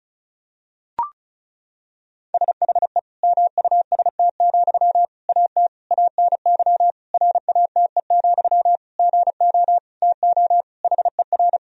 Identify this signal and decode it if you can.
Morse code, heard as a keyed tone, about 34 wpm, 700 Hz. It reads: SHE MUST, AT ANY RATE, GO TO HER
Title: Sentences from Top 1000 Words - Review Part 4 - 35wpm